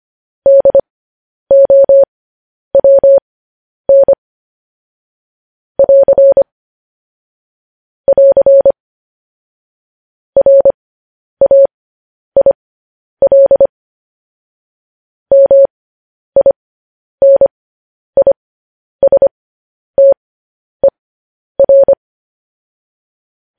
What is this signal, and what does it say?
Morse code, heard as a keyed tone, about 25 wpm, spaced as for 10 wpm, 550 Hz. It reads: DOWN <AR> <AR> RAIL MINISTER
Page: news headlines in morse code at 10 wpm